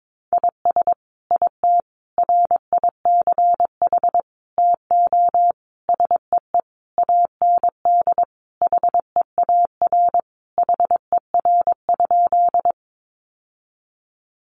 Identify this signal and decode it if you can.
Morse code, heard as a keyed tone, about 22 wpm, 700 Hz. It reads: IS IT RICH TO SEE AND HEAR HER?
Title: Sentences from Top 1100 Words 22wpm